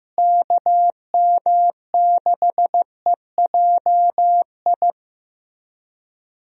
Morse code, heard as a keyed tone, about 15 wpm, 700 Hz. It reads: KM6EJI
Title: USA Call Signs - 6 Characters - Repeated 3x - 15wpm